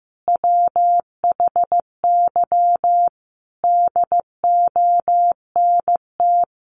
Morse code, heard as a keyed tone, about 15 wpm, 700 Hz. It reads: WHY DONT
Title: Sentences from Top 300 Words 15wpm